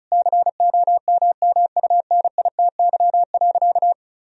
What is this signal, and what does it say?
Morse code, heard as a keyed tone, about 34 wpm, 700 Hz. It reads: COMMUNITY.
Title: Sentences from Top 1000 Words - Review Part 4 - 35wpm